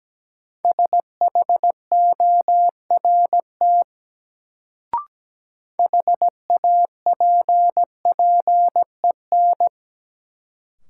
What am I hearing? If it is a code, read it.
Morse code, heard as a keyed tone, about 17 wpm, 700 Hz. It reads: SHORT HAPPEN
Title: Sentences from Top 400 Words - New Word Drill - 17wpm